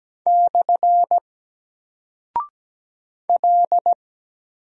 Morse code, heard as a keyed tone, about 17 wpm, 700 Hz. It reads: / L
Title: Single Character (Letter:Number:Punctuation:Prosign) 17wpm